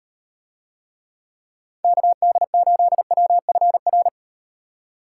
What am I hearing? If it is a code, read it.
Morse code, heard as a keyed tone, about 38 wpm, 700 Hz. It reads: KD8WFR